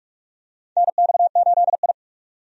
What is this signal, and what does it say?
Morse code, heard as a keyed tone, about 46 wpm, 700 Hz. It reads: NX8I
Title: USA Call Signs - 4 Characters - 45wpm